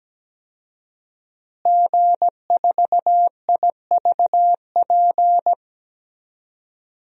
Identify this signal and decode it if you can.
Morse code, heard as a keyed tone, about 17 wpm, 700 Hz. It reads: G4IVP
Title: DXCC-100 Call Signs - 5 Characters - Repeated 3x - 17wpm